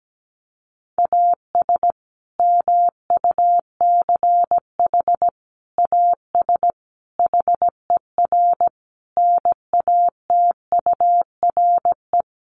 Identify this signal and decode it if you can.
Morse code, heard as a keyed tone, about 17 wpm, 700 Hz. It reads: AS MUCH AS HER NATURE
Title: Sentences from Top 700 Words 17wpm